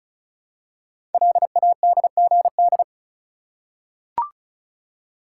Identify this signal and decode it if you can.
Morse code, heard as a keyed tone, about 36 wpm, 700 Hz. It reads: LADGD
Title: Code Groups - 5 Letters - 35wpm